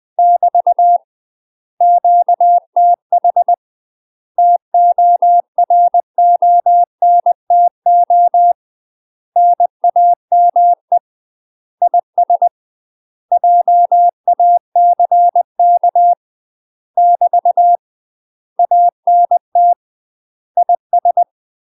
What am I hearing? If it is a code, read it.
Morse code, heard as a keyed tone, about 20 wpm, 700 Hz. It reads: <BT> QTH TORONTO NAME IS JACK <BT> ANT IS
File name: UK MASTER QSOs TRAINING / 20WPM / UK MASTER QSOs 0003